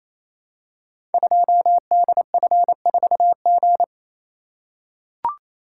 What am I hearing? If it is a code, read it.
Morse code, heard as a keyed tone, about 27 wpm, 700 Hz. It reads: 2DF4G
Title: Code Groups - 5 Letter:Numbers - 28wpm